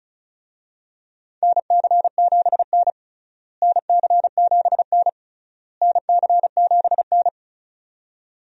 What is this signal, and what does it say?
Morse code, heard as a keyed tone, about 34 wpm, 700 Hz. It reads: NC7N NC7N NC7N
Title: USA Call Signs - 4 Characters - Repeated 3x - 35wpm